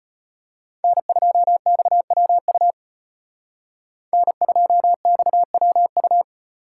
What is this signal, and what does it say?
Morse code, heard as a keyed tone, about 32 wpm, 700 Hz. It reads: N2XWU N2XWU